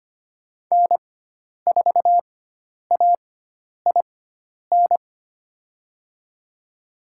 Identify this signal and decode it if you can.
Morse code, heard as a keyed tone, about 25 wpm, spaced as for 10 wpm, 700 Hz. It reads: N4AIN
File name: Lesson 16g - Letter c - Call Signs 25wpm